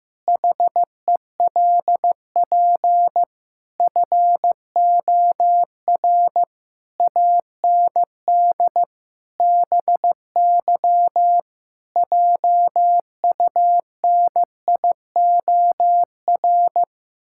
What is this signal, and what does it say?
Morse code, heard as a keyed tone, about 16 wpm, 700 Hz. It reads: HELP FOR AND BY JUNIOR
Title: Pangram Challenge v2 - 15wpm